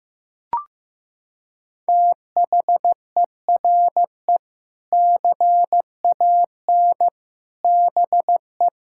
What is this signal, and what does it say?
Morse code, heard as a keyed tone, about 15 wpm, 700 Hz. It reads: THERE CAN BE
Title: Sentences from Top 1000 Words - Review Part 5 - 15wpm